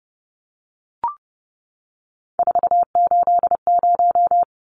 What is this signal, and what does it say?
Morse code, heard as a keyed tone, about 30 wpm, 700 Hz. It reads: 480
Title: Numbers - 3-Digit 30wpm